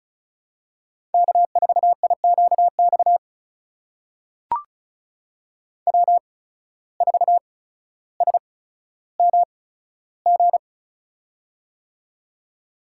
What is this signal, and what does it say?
Morse code, heard as a keyed tone, about 36 wpm, spaced as for 15 wpm, 700 Hz. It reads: K4IQX W4SMG
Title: USA Call Signs - 5 Characters - 8x Character Spacing - 35wpm